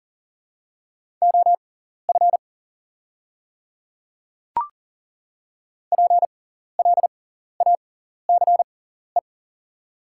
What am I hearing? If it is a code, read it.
Morse code, heard as a keyed tone, about 40 wpm, spaced as for 14 wpm, 700 Hz. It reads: OF PLACE
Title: 6x Character Spacing - Top 200 Words - 40wpm